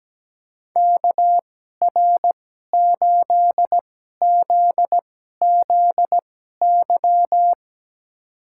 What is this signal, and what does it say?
Morse code, heard as a keyed tone, about 17 wpm, spaced as for 12 wpm, 700 Hz. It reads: KR8ZZY
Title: USA Call Signs - 6 Characters - 2x Character Spacing - 17wpm